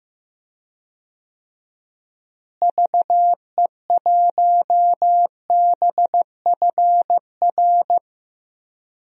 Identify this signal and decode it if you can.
Morse code, heard as a keyed tone, about 15 wpm, 700 Hz. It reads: VE1BFR
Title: DXCC-100 Call Signs - 6 Characters - Repeated 2x - 15wpm